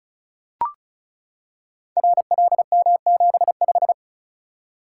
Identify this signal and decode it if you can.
Morse code, heard as a keyed tone, about 36 wpm, 700 Hz. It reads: RLM75
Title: Code Groups - 5 Letter:Numbers - 35wpm